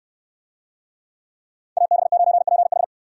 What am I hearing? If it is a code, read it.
Morse code, heard as a keyed tone, about 69 wpm, 700 Hz. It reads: AB9PH